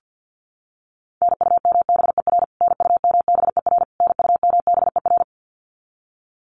Morse code, heard as a keyed tone, about 50 wpm, 700 Hz. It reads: NUMBER NUMBER NUMBER
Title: Top 500 Words Repeated 50wpm